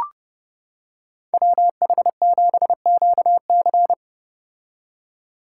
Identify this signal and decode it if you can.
Morse code, heard as a keyed tone, about 29 wpm, 700 Hz. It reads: WH7QC